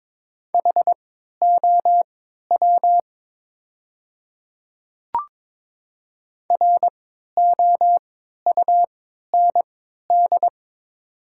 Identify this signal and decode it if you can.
Morse code, heard as a keyed tone, about 22 wpm, spaced as for 13 wpm, 700 Hz. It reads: HOW ROUND
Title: 3x Character Spacing - Top 200 Words - 22wpm